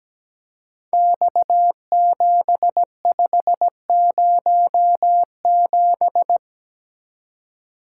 Morse code, heard as a keyed tone, about 17 wpm, 700 Hz. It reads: X7507
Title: Code Groups - 5 Letter:Numbers - 17wpm